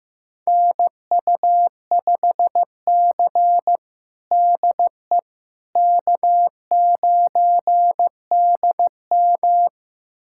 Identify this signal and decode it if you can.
Morse code, heard as a keyed tone, about 15 wpm, 700 Hz. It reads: NU5C DE K9DM
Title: USA Callsign DE Callsign - 4 Characters - 15wpm